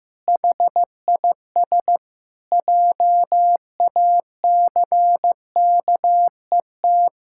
Morse code, heard as a keyed tone, about 15 wpm, 700 Hz. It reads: HIS JACKET
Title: Pangram Challenge v2 - 15wpm